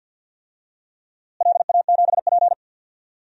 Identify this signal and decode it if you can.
Morse code, heard as a keyed tone, about 46 wpm, 700 Hz. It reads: LA7P
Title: DXCC-100 Call Signs - 4 Characters - Repeated 2x - 50wpm